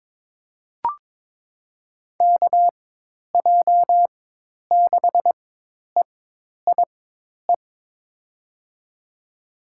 Morse code, heard as a keyed tone, about 21 wpm, spaced as for 10 wpm, 700 Hz. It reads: KJ6EIE
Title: USA Call Signs - 6 Characters - 4x Character Spacing - 22wpm